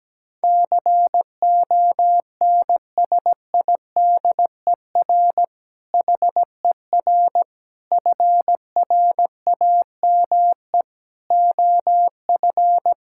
Morse code, heard as a keyed tone, about 17 wpm, 700 Hz. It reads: CONSIDER HER FRAME OF
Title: Sentences from Top 1000 Words - Review Part 6 - 17wpm